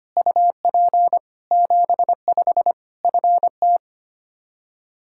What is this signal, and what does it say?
Morse code, heard as a keyed tone, about 25 wpm, 700 Hz. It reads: UP 75 FT
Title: K1USN SST QSO Practice 25wpm